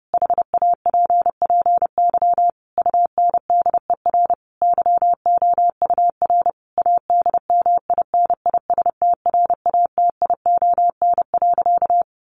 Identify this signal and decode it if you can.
Morse code, heard as a keyed tone, about 30 wpm, 700 Hz. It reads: HAPPY UNDER YOUR ADMINISTRATION.
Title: Sentences from Top 1500 Words 30wpm